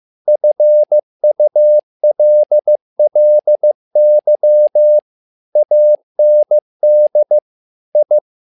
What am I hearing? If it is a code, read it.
Morse code, heard as a keyed tone, about 15 wpm, 600 Hz. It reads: FULLY AND I
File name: learning center / tower trouble / tower trouble 15wpm